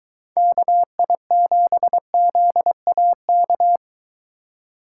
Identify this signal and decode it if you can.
Morse code, heard as a keyed tone, about 23 wpm, 700 Hz. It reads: KI7ZAK